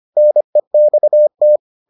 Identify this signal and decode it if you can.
Morse code, heard as a keyed tone, about 25 wpm, 600 Hz. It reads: NEXT